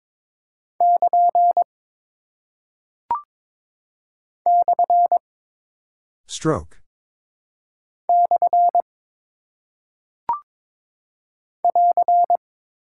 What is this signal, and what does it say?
Morse code, heard as a keyed tone, about 22 wpm, 700 Hz. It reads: <KN> / / <AR>
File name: practice / 22wpm / Single Punctuation-Prosign - 22wpm